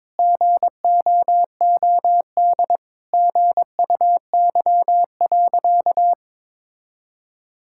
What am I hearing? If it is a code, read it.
Morse code, heard as a keyed tone, about 22 wpm, 700 Hz. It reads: GOOD GUY.
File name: practice / 22wpm / Sentences with 3-Word Phrases - 22wpm